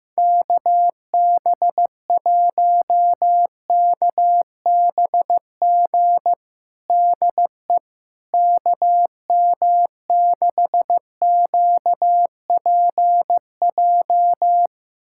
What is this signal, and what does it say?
Morse code, heard as a keyed tone, about 15 wpm, 700 Hz. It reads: KB1KBG DE KM6QPJ
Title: USA Callsign DE Callsign 15wpm